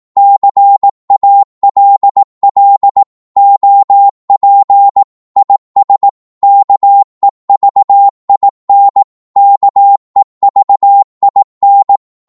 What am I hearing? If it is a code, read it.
Morse code, heard as a keyed tone, about 18 wpm, 800 Hz. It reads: CALL OP IS KEVIN KEVIN
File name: Morse / qso-18wpm